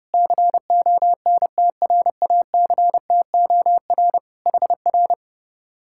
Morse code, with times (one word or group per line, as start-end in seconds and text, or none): 0.1-4.2 CONTRACTOR
4.5-5.1 HR